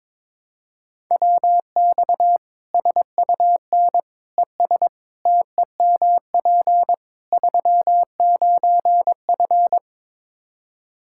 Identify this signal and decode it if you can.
Morse code, heard as a keyed tone, about 22 wpm, 700 Hz. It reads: WX SUN ES TEMP 39F
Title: Practice QSO - Exchange 2 - Info - 22wpm